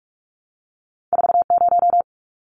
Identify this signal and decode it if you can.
Morse code, heard as a keyed tone, about 46 wpm, 700 Hz. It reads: 40
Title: Numbers - 2-Digit 45wpm